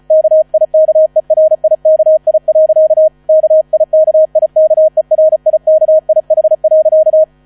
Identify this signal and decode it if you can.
Morse code, heard as a keyed tone, about 34 wpm, 630 Hz. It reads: KIKERIKI. KIKIKERIKIH.
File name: 35wpm